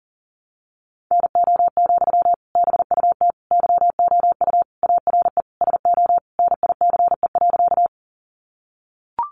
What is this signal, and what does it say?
Morse code, heard as a keyed tone, about 40 wpm, 700 Hz. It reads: NO, BUT YOU ARE SO NICE.
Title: Sentences from Top 1100 Words 40wpm